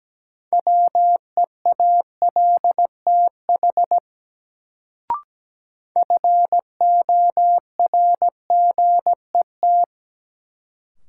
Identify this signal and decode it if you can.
Morse code, heard as a keyed tone, about 17 wpm, 700 Hz. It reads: WEALTH FORGET
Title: Sentences from Top 1200 Words - New Word Drill Alt - 17wpm